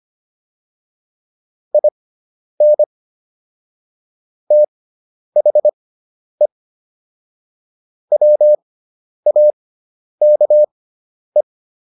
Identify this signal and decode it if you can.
Morse code, heard as a keyed tone, about 25 wpm, spaced as for 10 wpm, 600 Hz. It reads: IN THE WAKE
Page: intermediate-practice-files